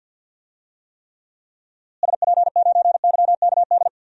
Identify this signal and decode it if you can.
Morse code, heard as a keyed tone, about 52 wpm, 700 Hz. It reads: SP9YKD
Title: DXCC-100 Call Signs - 6 Characters - Repeated 2x - 50wpm